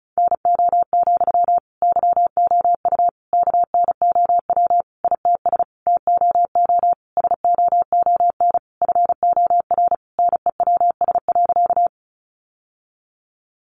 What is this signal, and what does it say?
Morse code, heard as a keyed tone, about 34 wpm, 700 Hz. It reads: NO, YOU KNOW ITS TOO SOON FOR NEWS.
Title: Sentences from Top 800 Words 35wpm